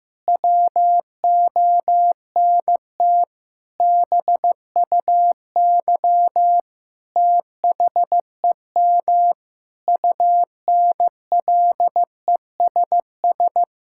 Morse code, heard as a keyed tone, about 15 wpm, 700 Hz. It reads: WONT BUY THEM UNLESS
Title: Sentences from Top 1000 Words - Review Part 6 - 15wpm